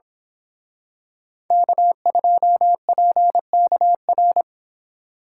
Morse code, heard as a keyed tone, about 27 wpm, 700 Hz. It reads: K2PKR
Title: Speed Racing - USA 1x3 Call Signs 22wpm